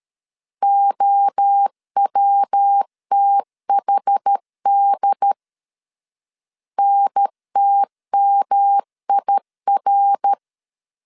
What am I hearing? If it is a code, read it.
Morse code, heard as a keyed tone, about 12 wpm, 800 Hz. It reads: OWTHD NTMIR
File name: CW / DET / 1 3 1500C S B 12wpm 2828